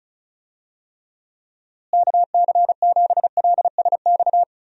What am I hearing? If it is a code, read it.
Morse code, heard as a keyed tone, about 36 wpm, 700 Hz. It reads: KC7LSX
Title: USA Call Signs - 6 Characters - Repeated 3x - 35wpm